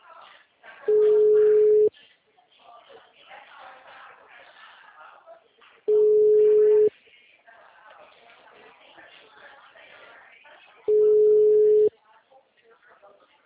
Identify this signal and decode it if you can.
Ringback tone (Europe, 425 Hz), with speech.